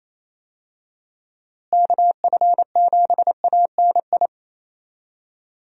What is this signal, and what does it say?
Morse code, heard as a keyed tone, about 28 wpm, 700 Hz. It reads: KF7ANI
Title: USA Call Signs - 6 Characters - Repeated 2x - 28wpm